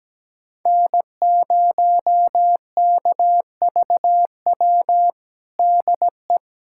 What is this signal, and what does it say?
Morse code, heard as a keyed tone, about 17 wpm, 700 Hz. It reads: N0KVW DE
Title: USA Callsign DE Callsign - 5 Characters - 17wpm